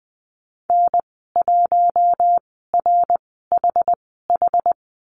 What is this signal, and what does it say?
Morse code, heard as a keyed tone, about 20 wpm, spaced as for 14 wpm, 700 Hz. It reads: N1RHH
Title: 082 Lesson 14g - Number 2 - Call Signs 20wpm